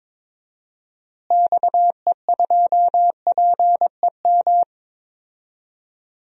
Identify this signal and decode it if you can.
Morse code, heard as a keyed tone, about 22 wpm, 700 Hz. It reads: XE2PEM
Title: DXCC-100 Call Signs - 6 Characters - Repeated 3x - 22wpm